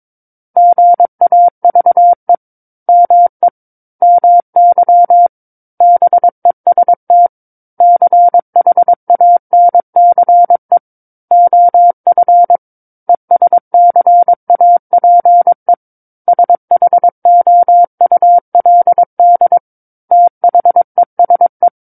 Morse code, heard as a keyed tone, about 23 wpm, 700 Hz. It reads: GAVE ME MY BEST CHANCE OF ESCAPE SHOULD THESE
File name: Book One / 13-Chapter 13 - 22 WPM